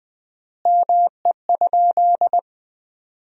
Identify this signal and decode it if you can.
Morse code, heard as a keyed tone, about 21 wpm, 700 Hz. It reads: ME?
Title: 4x Word-Spacing - Sentences from Top 200 Words - 20wpm